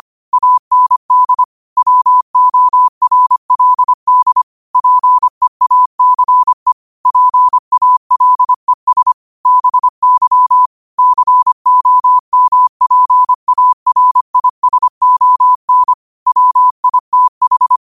Morse code, heard as a keyed tone, about 25 wpm, 1000 Hz. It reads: AND WORLD PEACE PALES BY COMPARISON WITH
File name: qotdfiles / 25wpm